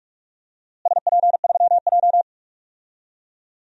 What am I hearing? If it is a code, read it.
Morse code, heard as a keyed tone, about 46 wpm, 700 Hz. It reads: SP3J
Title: DXCC-100 Call Signs - 4 Characters - Repeated 3x - 45wpm